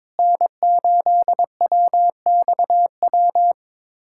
Morse code, heard as a keyed tone, about 21 wpm, 700 Hz. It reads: N8WXW
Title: USA Callsign DE Callsign - 5 Characters - 22wpm